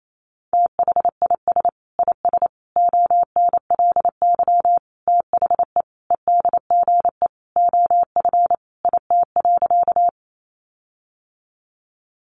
Morse code, heard as a keyed tone, about 28 wpm, 700 Hz. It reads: THIS IS ONLY THE EDGE OF IT.
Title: Sentences from Top 1300 Words 28wpm